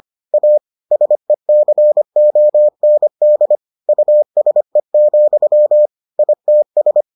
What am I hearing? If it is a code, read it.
Morse code, heard as a keyed tone, about 25 wpm, 600 Hz. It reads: A SECOND USE, ITS